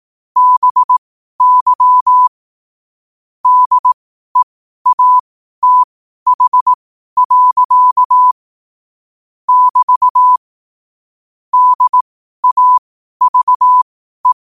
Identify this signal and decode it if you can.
Morse code, heard as a keyed tone, about 18 wpm, spaced as for 12 wpm, 1000 Hz. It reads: BY DEATH. <BT> DAVE
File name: qotdfiles / 12wpm